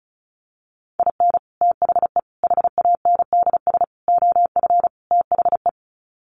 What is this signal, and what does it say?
Morse code, heard as a keyed tone, about 34 wpm, 700 Hz. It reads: IN THE HANDS OF THE
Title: Sets of 5 Words 35wpm